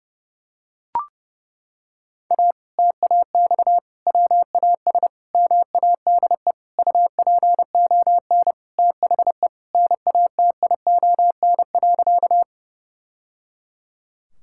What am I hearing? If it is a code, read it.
Morse code, heard as a keyed tone, about 30 wpm, 700 Hz. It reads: A TAX WAS MADE UPON THE NATION.
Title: Sentences from Top 1000 Words - Review Part 6 - 30wpm